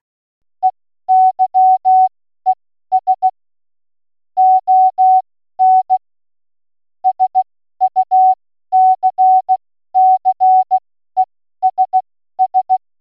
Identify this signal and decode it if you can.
Morse code, heard as a keyed tone, about 16 wpm, spaced as for 12 wpm, 750 Hz. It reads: EYES ON SUCCESS